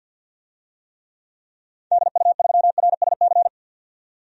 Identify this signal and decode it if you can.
Morse code, heard as a keyed tone, about 48 wpm, 700 Hz. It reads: DU3RSC
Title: DXCC-100 Call Signs - 6 Characters - 50wpm